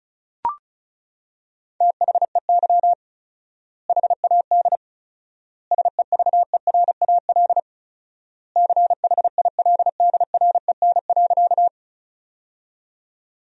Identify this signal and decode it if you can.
Morse code, heard as a keyed tone, about 36 wpm, 700 Hz. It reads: THEY HAD SEVERAL CHILDREN.
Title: 4x Word-Spacing - Sentences from Top 400 Words - 35wpm